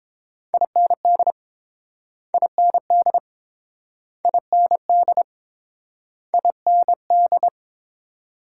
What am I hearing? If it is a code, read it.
Morse code, heard as a keyed tone, about 30 wpm, 700 Hz. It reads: IND IND IND IND